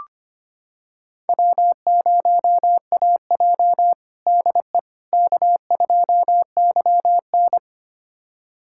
Morse code, heard as a keyed tone, about 25 wpm, 700 Hz. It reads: W0AJ DE K2YN
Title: USA Callsign DE Callsign - 4 Characters - 25wpm